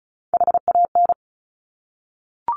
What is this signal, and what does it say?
Morse code, heard as a keyed tone, about 36 wpm, 700 Hz. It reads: HAN
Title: Top Three Letter Combinations in Words 35wpm